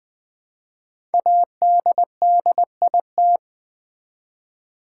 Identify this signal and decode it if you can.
Morse code, heard as a keyed tone, about 20 wpm, 700 Hz. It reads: ADDIT
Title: Build A Word - 5 Letters At A Time - 10-Letter Words - 20wpm